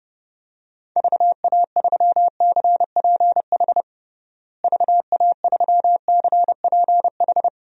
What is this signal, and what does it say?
Morse code, heard as a keyed tone, about 30 wpm, 700 Hz. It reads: VA3CPH VA3CPH
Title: DXCC-100 Call Signs - 6 Characters - Repeated 3x - 30wpm